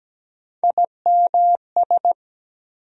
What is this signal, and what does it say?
Morse code, heard as a keyed tone, about 17 wpm, 700 Hz. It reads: IMS
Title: USA Call Sign Postfixes - 3 Letters 17wpm